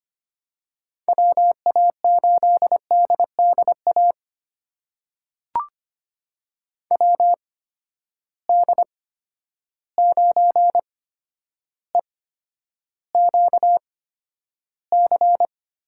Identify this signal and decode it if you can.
Morse code, heard as a keyed tone, about 25 wpm, spaced as for 11 wpm, 700 Hz. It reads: WA8DDA WD9EQC
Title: USA Call Signs - 6 Characters - 8x Character Spacing - 25wpm